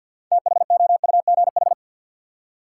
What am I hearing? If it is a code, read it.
Morse code, heard as a keyed tone, about 51 wpm, 700 Hz. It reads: THOUGH
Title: Six Letter Words - 50wpm